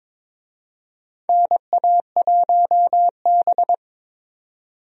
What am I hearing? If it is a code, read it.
Morse code, heard as a keyed tone, about 22 wpm, 700 Hz. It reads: NA1B
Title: USA Call Signs - 4 Characters - 22wpm